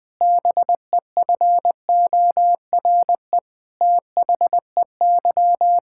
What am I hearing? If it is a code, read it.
Morse code, heard as a keyed tone, about 20 wpm, 700 Hz. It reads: BEFORE THEY
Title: Sentences with 7-Word Phrases - Spoken Beginning - 20wpm